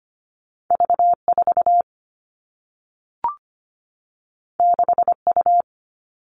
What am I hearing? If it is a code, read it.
Morse code, heard as a keyed tone, about 25 wpm, 700 Hz. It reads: V4 6U
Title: All Cut Numbers - 2-Digit 25wpm